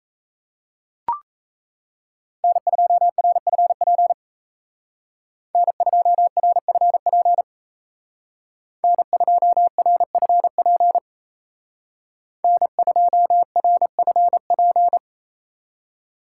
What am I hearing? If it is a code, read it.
Morse code, heard as a keyed tone, about 33 wpm, 700 Hz. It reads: N2RFP N2RFP N2RFP N2RFP
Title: Speed Racing - USA 1x3 Call Signs 28wpm